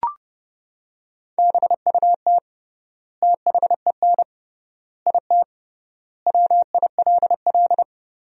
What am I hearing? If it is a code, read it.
Morse code, heard as a keyed tone, about 30 wpm, 700 Hz. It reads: BUT THEN IT WILL
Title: 3x Word-Spacing - Sentences from Top 300 Words - 30wpm